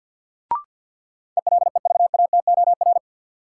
Morse code, heard as a keyed tone, about 52 wpm, 700 Hz. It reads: ELEVATOR
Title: Sentences from Top 1600 Words - New Word Drill - 50wpm